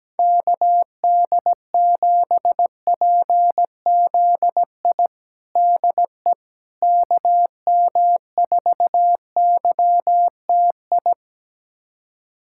Morse code, heard as a keyed tone, about 17 wpm, 700 Hz. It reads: KD7PZI DE KM4YTI
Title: USA Callsign DE Callsign - 6 Characters - 17wpm